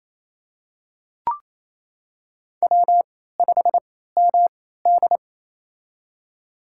Morse code, heard as a keyed tone, about 28 wpm, spaced as for 16 wpm, 700 Hz. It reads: W5MD
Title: USA Call Signs - 4 Characters - 3x Character Spacing - 28wpm